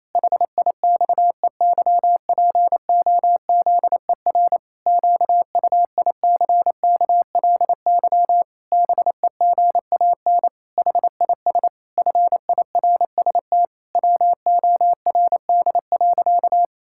Morse code, heard as a keyed tone, about 28 wpm, 700 Hz. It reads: HIXEYPOZER QUICKLY BEGAN HIS FIRST WORD.
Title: Pangram Challenge v2 - 28wpm